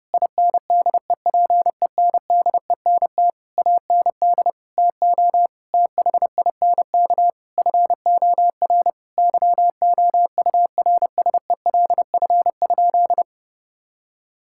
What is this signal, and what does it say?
Morse code, heard as a keyed tone, about 30 wpm, 700 Hz. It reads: INDEPENDENT AND TO THINK FOR YOURSELF?
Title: Sentences from Top 1000 Words - Review Part 6 - 30wpm